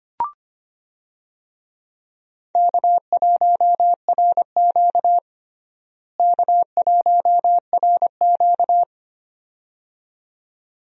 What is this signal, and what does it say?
Morse code, heard as a keyed tone, about 25 wpm, 700 Hz. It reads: K1RQ K1RQ
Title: USA Call Signs - 4 Characters - Repeated 2x - 25wpm